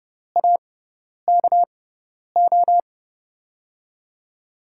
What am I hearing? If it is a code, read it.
Morse code, heard as a keyed tone, about 30 wpm, spaced as for 10 wpm, 700 Hz. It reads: AKO
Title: USA Call Signs - 6x Character Spacing - 30wpm